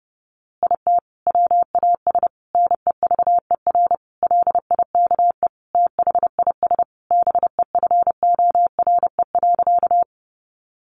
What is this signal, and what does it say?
Morse code, heard as a keyed tone, about 30 wpm, 700 Hz. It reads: IT WAS NEVER LIKE THIS BEFORE.
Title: A Sound Of Thunder 30wpm